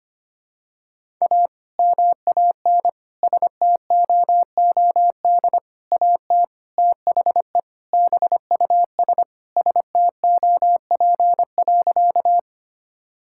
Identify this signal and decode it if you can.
Morse code, heard as a keyed tone, about 25 wpm, 700 Hz. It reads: A MAN STOOD AT THE BUS STOP.
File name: practice / 25wpm / Sentences with 2-Word Phrases - 25wpm